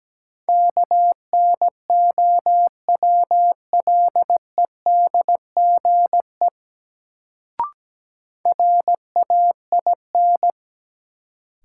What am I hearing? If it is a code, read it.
Morse code, heard as a keyed tone, about 17 wpm, 700 Hz. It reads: KNOWLEDGE RAIN
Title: Sentences from Top 600 Words - New Word Drill - 17wpm